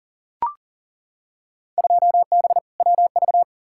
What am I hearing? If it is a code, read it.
Morse code, heard as a keyed tone, about 40 wpm, 700 Hz. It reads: 2B WV
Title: Field Day Exchanges Encore - 40wpm